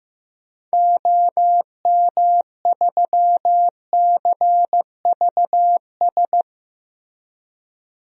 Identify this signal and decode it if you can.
Morse code, heard as a keyed tone, about 15 wpm, 700 Hz. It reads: OM3CVS
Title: DXCC-100 Call Signs - 6 Characters - Repeated 3x - 15wpm